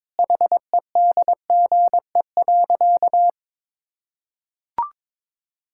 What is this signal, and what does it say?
Morse code, heard as a keyed tone, about 21 wpm, 700 Hz. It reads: HEDGE.
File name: practice / 22wpm / Easy Sentences with Hints - 22wpm